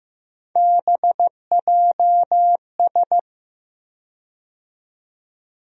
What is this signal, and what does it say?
Morse code, heard as a keyed tone, about 16 wpm, 700 Hz. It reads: BJS
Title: USA Call Sign Postfixes - 3 Letters 15wpm